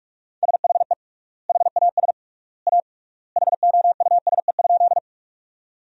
Morse code, heard as a keyed tone, about 41 wpm, 700 Hz. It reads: SHE HAS A HOUSE?
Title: 3x Word-Spacing - Sentences from Top 200 Words - 45wpm